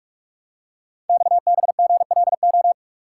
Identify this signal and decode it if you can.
Morse code, heard as a keyed tone, about 44 wpm, 700 Hz. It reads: XBGLO
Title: Code Groups - 5 Letter:Numbers - 45wpm